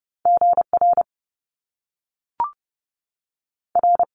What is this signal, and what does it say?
Morse code, heard as a keyed tone, about 30 wpm, 700 Hz. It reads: GR R
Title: Top Two Letter Combinations in Words 30wpm